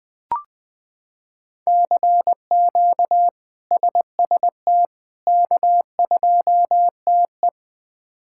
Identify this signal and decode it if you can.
Morse code, heard as a keyed tone, about 20 wpm, 700 Hz. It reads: CQ SST K2TE
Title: K1USN SST QSO Practice - 20wpm